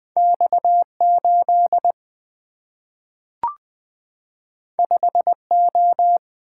Morse code, heard as a keyed tone, about 20 wpm, 700 Hz. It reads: X 8 5 O
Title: Single-Letter Single-Number 20wpm